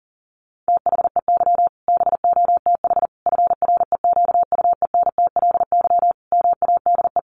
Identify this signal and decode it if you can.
Morse code, heard as a keyed tone, about 40 wpm, 700 Hz. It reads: THEY BOTH FREQUENTLY MADE
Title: Sentences from Top 1300 Words 40wpm